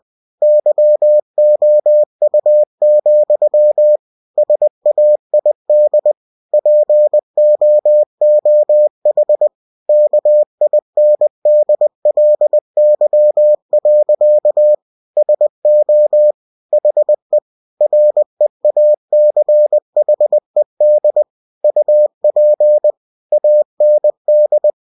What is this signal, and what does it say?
Morse code, heard as a keyed tone, about 20 wpm, 600 Hz. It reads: YOU, SAID POOH KINDLY. SO HE REACHED UP AND
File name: cw-books / winnie-the-pooh / winnie-20wpm-page-0048